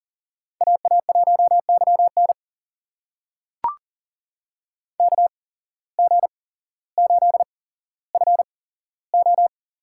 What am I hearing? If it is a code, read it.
Morse code, heard as a keyed tone, about 39 wpm, spaced as for 17 wpm, 700 Hz. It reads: AA1YN KG8FO
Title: USA Call Signs - 5 Characters - 8x Character Spacing - 40wpm